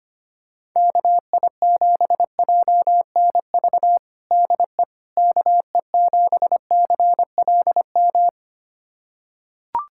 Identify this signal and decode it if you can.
Morse code, heard as a keyed tone, about 25 wpm, 700 Hz. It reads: KI7JNV DE KE7CLM
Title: USA Callsign DE Callsign - 6 Characters - 25wpm